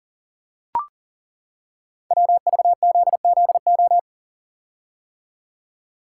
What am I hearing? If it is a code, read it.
Morse code, heard as a keyed tone, about 38 wpm, 690 Hz. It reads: WVZZO
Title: Code Groups - 5 Letters - 40wpm